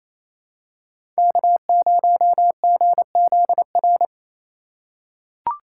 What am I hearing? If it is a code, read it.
Morse code, heard as a keyed tone, about 28 wpm, 700 Hz. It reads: K0GZR
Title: USA Call Signs - 5 Characters - 4x Character Spacing - 28wpm